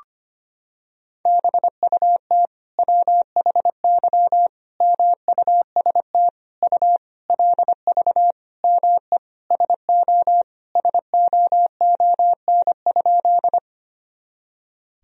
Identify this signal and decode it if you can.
Morse code, heard as a keyed tone, about 25 wpm, 700 Hz. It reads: BUT WHY MUST U LV ME SO SOON?
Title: Sentences from Top 400 Words and Abbreviations 25wpm